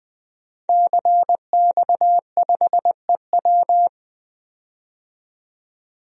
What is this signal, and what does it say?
Morse code, heard as a keyed tone, about 20 wpm, 700 Hz. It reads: CX5EW
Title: DXCC-100 Call Signs - 5 Characters - 20wpm